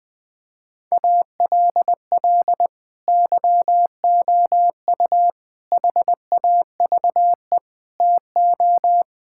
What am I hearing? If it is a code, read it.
Morse code, heard as a keyed tone, about 21 wpm, 700 Hz. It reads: ALL YOU HAVE TO
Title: Sentences with 6-Word Phrases - Spoken Beginning - 20wpm